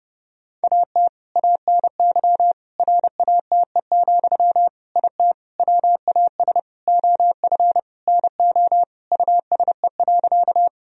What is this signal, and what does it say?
Morse code, heard as a keyed tone, about 32 wpm, 700 Hz. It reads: AT ANY RATE, IT WAS OF NO USE.
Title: Sentences from Top 1000 Words - Review Part 4 - 30wpm